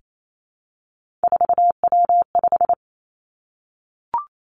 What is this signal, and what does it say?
Morse code, heard as a keyed tone, about 28 wpm, 700 Hz. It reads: 4W5
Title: Three Challenging Letter-Number Combinations 28wpm